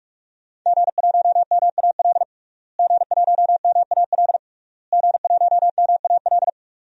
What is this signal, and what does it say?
Morse code, heard as a keyed tone, about 44 wpm, 700 Hz. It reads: G1MAL G1MAL G1MAL
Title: DXCC-100 Call Signs - 5 Characters - Repeated 3x - 45wpm